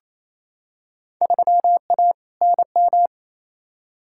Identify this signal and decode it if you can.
Morse code, heard as a keyed tone, about 28 wpm, 700 Hz. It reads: 3A NM
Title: Field Day Exchanges Encore - 28wpm